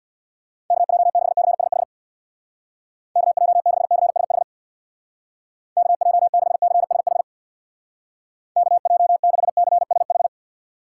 Morse code, heard as a keyed tone, about 64 wpm, 700 Hz. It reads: KJ6CSH KJ6CSH KJ6CSH KJ6CSH